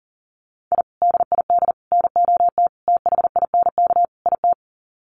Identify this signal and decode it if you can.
Morse code, heard as a keyed tone, about 40 wpm, 700 Hz. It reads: I DID NOT THINK IT
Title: Sets of 5 Words Encore2 40wpm